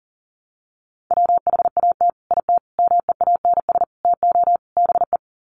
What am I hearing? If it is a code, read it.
Morse code, heard as a keyed tone, about 40 wpm, 700 Hz. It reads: WHAT IT MEANS TO BE
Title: Sets of 5 Words 40wpm